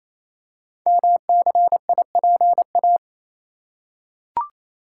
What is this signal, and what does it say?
Morse code, heard as a keyed tone, about 27 wpm, 700 Hz. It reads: MCIPA
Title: Code Groups - 5 Letters - 28wpm